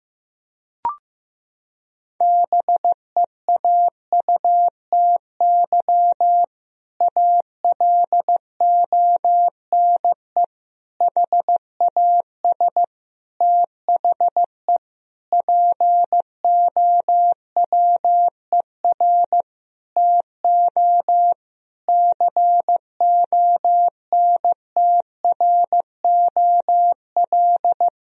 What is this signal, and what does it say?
Morse code, heard as a keyed tone, about 16 wpm, 700 Hz. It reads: BEAUTY ALONE HAS THE POWER TO CONTROL
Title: Sentences from Top 1000 Words - Review Part 2 - 15wpm